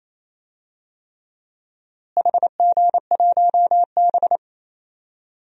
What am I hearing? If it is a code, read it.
Morse code, heard as a keyed tone, about 28 wpm, 700 Hz. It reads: HG1B
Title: DXCC-100 Call Signs - 4 Characters - Repeated 3x - 28wpm